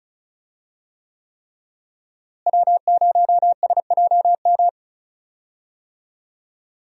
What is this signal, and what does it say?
Morse code, heard as a keyed tone, about 34 wpm, 700 Hz. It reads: W0SJM
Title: USA Call Signs - 5 Characters - Repeated 3x - 35wpm